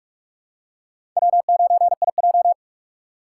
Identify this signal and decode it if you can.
Morse code, heard as a keyed tone, about 46 wpm, 700 Hz. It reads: W9IJ
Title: USA Call Signs - 4 Characters - 3x Character Spacing - 45wpm